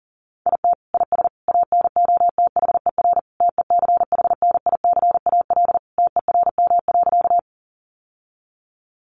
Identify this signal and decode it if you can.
Morse code, heard as a keyed tone, about 40 wpm, 700 Hz. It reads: IT IS ANOTHER TECHNICAL TERM.
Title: Sentences from Top 1100 Words 40wpm